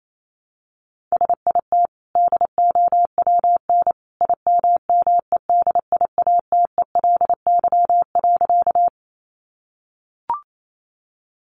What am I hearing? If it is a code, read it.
Morse code, heard as a keyed tone, about 28 wpm, 700 Hz. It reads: SIT DOWN IMMEDIATELY.
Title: Sentences from Top 1100 Words 28wpm